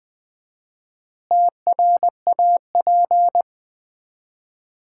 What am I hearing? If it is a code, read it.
Morse code, heard as a keyed tone, about 21 wpm, 700 Hz. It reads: TRAP
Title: Common QSO Phrases - 4x Word-Spacing - 20wpm